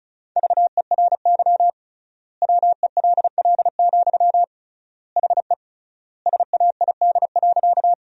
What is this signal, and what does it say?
Morse code, heard as a keyed tone, about 36 wpm, 700 Hz. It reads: VERY WELL, HE SAID.
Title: 3x Word-Spacing - Sentences from Top 200 Words - 35wpm